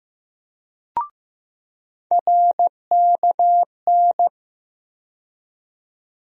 Morse code, heard as a keyed tone, about 15 wpm, 700 Hz. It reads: RKN
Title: USA Call Sign Postfixes - 3 Letters 15wpm